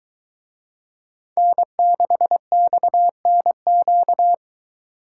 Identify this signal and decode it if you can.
Morse code, heard as a keyed tone, about 21 wpm, 700 Hz. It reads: N6XNQ